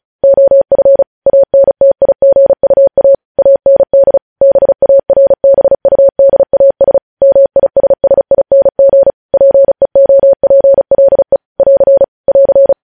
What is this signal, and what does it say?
Morse code, heard as a keyed tone, about 36 wpm, 550 Hz. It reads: OF ANTIGUA AND BARBUDAS MISSING PEOPLE <AR> <AR>
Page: News headlines in Morse code at 35 wpm